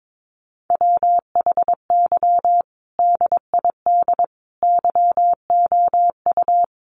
Morse code, heard as a keyed tone, about 22 wpm, 700 Hz. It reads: WHY DID YOU
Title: Sentences from Top 1500 Words 22wpm